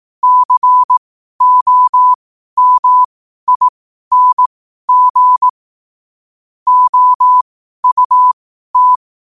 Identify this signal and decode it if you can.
Morse code, heard as a keyed tone, about 18 wpm, spaced as for 12 wpm, 1000 Hz. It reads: COMING OUT
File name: qotdfiles / 12wpm